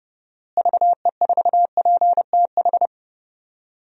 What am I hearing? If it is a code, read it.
Morse code, heard as a keyed tone, about 30 wpm, 700 Hz. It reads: VE4PTH
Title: DXCC-100 Call Signs - 6 Characters - Repeated 2x - 30wpm